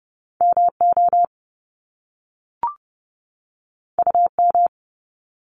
Morse code, heard as a keyed tone, about 30 wpm, 700 Hz. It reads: MO UM
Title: Top Two Letter Combinations in Words 30wpm